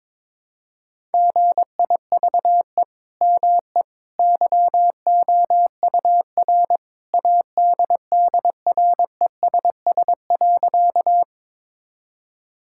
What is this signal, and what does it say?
Morse code, heard as a keyed tone, about 22 wpm, 700 Hz. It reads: GIVE ME YOUR ADDRESS.
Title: Sentences from Top 1000 Words - Review Part 6 - 22wpm